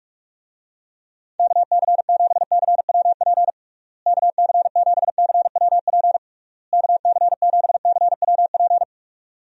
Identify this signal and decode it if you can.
Morse code, heard as a keyed tone, about 46 wpm, 700 Hz. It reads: KC7CWP KC7CWP KC7CWP
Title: USA Call Signs - 6 Characters - Repeated 3x - 45wpm